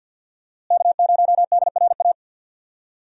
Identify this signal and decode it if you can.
Morse code, heard as a keyed tone, about 48 wpm, 700 Hz. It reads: K0DRA
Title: USA Call Signs - 5 Characters - Repeated 2x - 50wpm